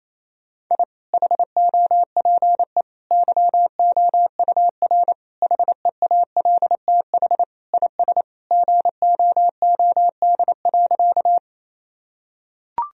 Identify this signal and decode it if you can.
Morse code, heard as a keyed tone, about 28 wpm, 700 Hz. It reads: I HOPE YOUR HEALTH IS GOOD.
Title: Sentences from Top 1000 Words - Review Part 3 - 28wpm